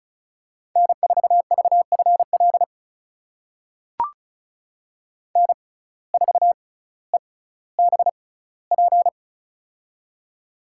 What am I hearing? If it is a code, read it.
Morse code, heard as a keyed tone, about 36 wpm, spaced as for 18 wpm, 700 Hz. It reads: N4VFL N4EBP
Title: USA Call Signs - 5 Characters - 6x Character Spacing - 35wpm